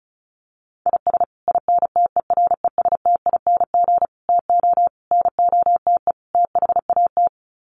Morse code, heard as a keyed tone, about 34 wpm, 700 Hz. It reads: IS INTERESTING TO NOTE THAT
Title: Sets of 5 Words Encore2 35wpm